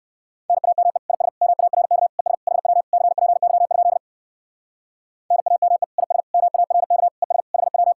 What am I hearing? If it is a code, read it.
Morse code, heard as a keyed tone, about 59 wpm, 700 Hz. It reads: NAME IS MARK ES HW CPY? NAME IS MARK ES HW